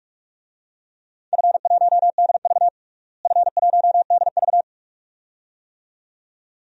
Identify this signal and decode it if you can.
Morse code, heard as a keyed tone, about 46 wpm, 700 Hz. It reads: F1DV F1DV
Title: DXCC-100 Call Signs - 4 Characters - Repeated 2x - 45wpm